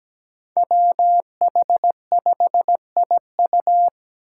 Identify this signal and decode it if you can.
Morse code, heard as a keyed tone, about 17 wpm, 700 Hz. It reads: WH5IU
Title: Five Challenging Letter-Number Combinations 17wpm